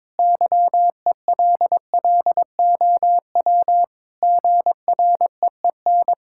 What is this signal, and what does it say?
Morse code, heard as a keyed tone, about 22 wpm, 700 Hz. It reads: YELLOW GREEN
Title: Pangram Challenge v2 - 22wpm